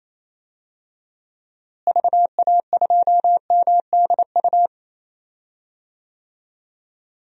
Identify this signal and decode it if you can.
Morse code, heard as a keyed tone, about 28 wpm, 700 Hz. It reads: VA2MDU
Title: DXCC-100 Call Signs - 6 Characters - Repeated 3x - 28wpm